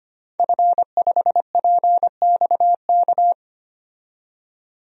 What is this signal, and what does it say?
Morse code, heard as a keyed tone, about 25 wpm, 700 Hz. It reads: F5PXK
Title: DXCC-100 Call Signs - 5 Characters - 25wpm